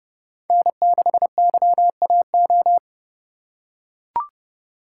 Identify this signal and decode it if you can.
Morse code, heard as a keyed tone, about 30 wpm, 700 Hz. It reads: N6YAO
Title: USA Call Signs - 5 Characters - 2x Character Spacing - 30wpm